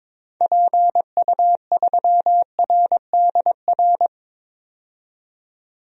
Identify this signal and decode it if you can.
Morse code, heard as a keyed tone, about 22 wpm, 700 Hz. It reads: PU3RDR
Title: DXCC-100 Call Signs - 6 Characters - 22wpm